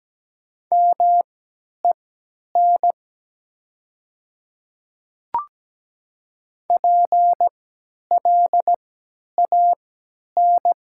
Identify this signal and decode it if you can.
Morse code, heard as a keyed tone, about 17 wpm, spaced as for 10 wpm, 700 Hz. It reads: MEN PLAN
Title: 3x Character Spacing - Top 500 Words - 17wpm